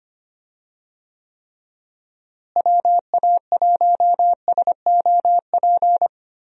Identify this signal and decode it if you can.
Morse code, heard as a keyed tone, about 25 wpm, 700 Hz. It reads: WA1SOP
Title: USA Call Signs - 6 Characters - Repeated 2x - 25wpm